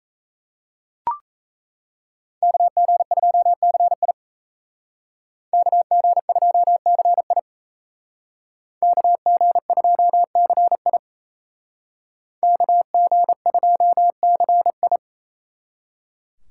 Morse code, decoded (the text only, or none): KG2CI KG2CI KG2CI KG2CI